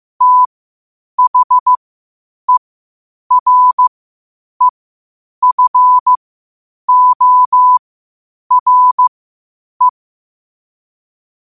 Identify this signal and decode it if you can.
Morse code, heard as a keyed tone, about 15 wpm, spaced as for 8 wpm, 1000 Hz. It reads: THEREFORE
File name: qotdfiles / 8wpm